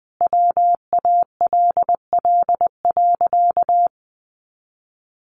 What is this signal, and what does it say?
Morse code, heard as a keyed tone, about 20 wpm, 700 Hz. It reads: WALL.
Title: Sentences from Top 1500 Words 20wpm